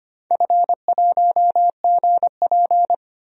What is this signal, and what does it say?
Morse code, heard as a keyed tone, about 25 wpm, 700 Hz. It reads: F1GP
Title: DXCC-100 Call Signs - 4 Characters - 25wpm